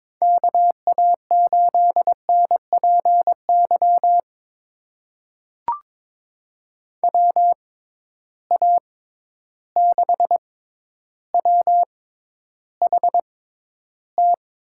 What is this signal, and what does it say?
Morse code, heard as a keyed tone, about 22 wpm, spaced as for 11 wpm, 700 Hz. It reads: KA8NPY WA6WHT
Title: USA Call Signs - 6 Characters - 6x Character Spacing - 22wpm